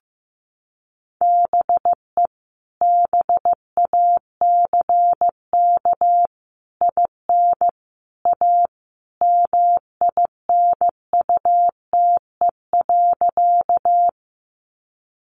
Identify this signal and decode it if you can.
Morse code, heard as a keyed tone, about 15 wpm, 700 Hz. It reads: BE BACK IN A MINUTE.
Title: Sentences from Top 500 Words 15wpm